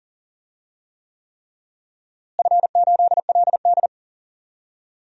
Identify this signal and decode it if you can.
Morse code, heard as a keyed tone, about 40 wpm, 700 Hz. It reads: F8LD
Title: DXCC-100 Call Signs - 4 Characters - Repeated 3x - 40wpm